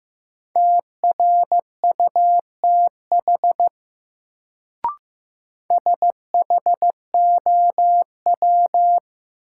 Morse code, heard as a keyed tone, about 15 wpm, 700 Hz. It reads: TRUTH SHOW
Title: Sets of 5 Words - New Word Drill - 15wpm